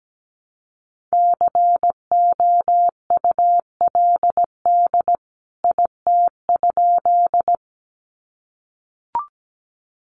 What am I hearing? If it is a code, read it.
Morse code, heard as a keyed tone, about 17 wpm, 700 Hz. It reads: COULD IT?
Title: A Sound Of Thunder 17wpm